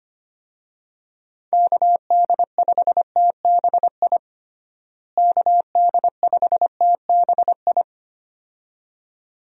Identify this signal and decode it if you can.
Morse code, heard as a keyed tone, about 25 wpm, 700 Hz. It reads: KD5TBI KD5TBI
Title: USA Call Signs - 6 Characters - Repeated 2x - 25wpm